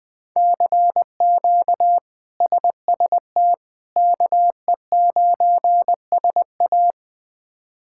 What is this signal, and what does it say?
Morse code, heard as a keyed tone, about 20 wpm, 700 Hz. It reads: CQ SST KE9SA
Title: K1USN SST QSO Practice - 20wpm